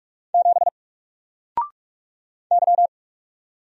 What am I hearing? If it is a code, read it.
Morse code, heard as a keyed tone, about 44 wpm, 700 Hz. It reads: 7 Y
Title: Single Letter-Number 45wpm